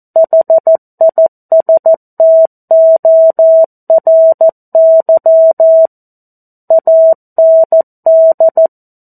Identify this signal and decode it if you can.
Morse code, heard as a keyed tone, about 14 wpm, 650 Hz. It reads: HISTORY AND
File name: 02-Chapter-02 14 WPM